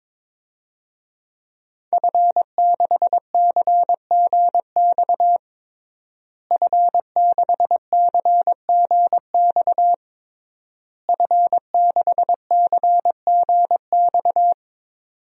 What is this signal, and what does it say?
Morse code, heard as a keyed tone, about 22 wpm, 700 Hz. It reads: F6CGX F6CGX F6CGX
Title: DXCC-100 Call Signs - 5 Characters - Repeated 3x - 22wpm